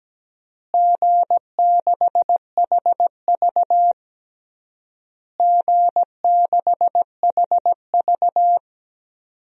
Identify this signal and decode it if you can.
Morse code, heard as a keyed tone, about 17 wpm, 700 Hz. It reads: G6HV G6HV
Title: DXCC-100 Call Signs - 4 Characters - Repeated 3x - 17wpm